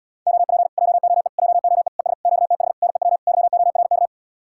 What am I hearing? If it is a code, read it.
Morse code, heard as a keyed tone, about 77 wpm, 700 Hz. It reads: OP JOE JOE ES QTH NEW YORK